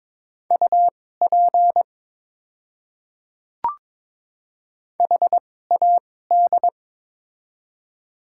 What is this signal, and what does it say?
Morse code, heard as a keyed tone, about 22 wpm, spaced as for 16 wpm, 690 Hz. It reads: UP HAD
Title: 2x Character Spacing - Top 100 Words - 22wpm